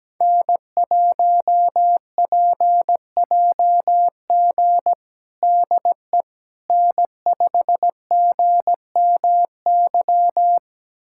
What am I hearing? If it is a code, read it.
Morse code, heard as a keyed tone, about 17 wpm, 700 Hz. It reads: N1PJG DE N5GMY
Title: USA Callsign DE Callsign - 5 Characters - 17wpm